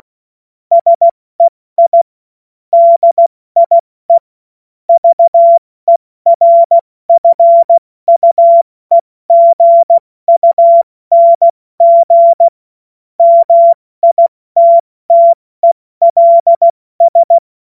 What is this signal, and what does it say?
Morse code, heard as a keyed tone, about 16 wpm, 680 Hz. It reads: SEI DIE VERFUEGUNG MITTELS
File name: Morsekurs / Witze / Witz 25. Oktober 2022 70 F80